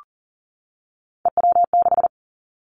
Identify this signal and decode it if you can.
Morse code, heard as a keyed tone, about 40 wpm, 700 Hz. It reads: EW6
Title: Callsign Prefixes for Top 100 DXCC - 40wpm